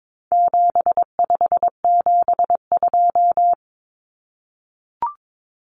Morse code, heard as a keyed tone, about 22 wpm, 700 Hz. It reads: 7572
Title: Numbers - 4-Digit 22wpm